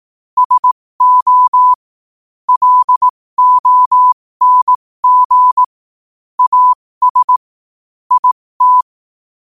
Morse code, heard as a keyed tone, about 18 wpm, spaced as for 15 wpm, 1000 Hz. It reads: SO LONG AS IT
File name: qotdfiles / 15wpm